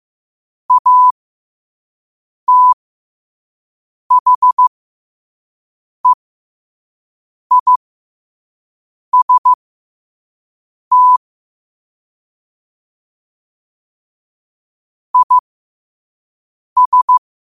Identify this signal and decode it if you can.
Morse code, heard as a keyed tone, about 15 wpm, spaced as for 5 wpm, 1000 Hz. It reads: ATHEIST IS
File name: qotdfiles / 5wpm